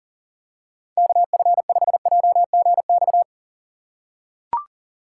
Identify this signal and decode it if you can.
Morse code, heard as a keyed tone, about 39 wpm, 700 Hz. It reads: KF5JGX
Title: USA Call Signs - 6 Characters - 3x Character Spacing - 40wpm